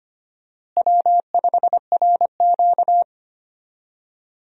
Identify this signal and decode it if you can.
Morse code, heard as a keyed tone, about 25 wpm, 700 Hz. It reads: W5RQ
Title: USA Call Signs - 4 Characters - Repeated 2x - 25wpm